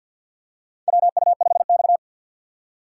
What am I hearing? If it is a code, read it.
Morse code, heard as a keyed tone, about 51 wpm, 700 Hz. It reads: WU5X
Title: USA Call Signs - 4 Characters - 3x Character Spacing - 50wpm